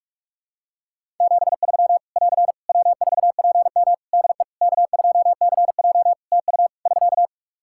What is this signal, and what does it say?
Morse code, heard as a keyed tone, about 41 wpm, 700 Hz. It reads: 73 <AR> W4PM DE K2CJ TU <SK>
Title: K1USN SST QSO Practice 45wpm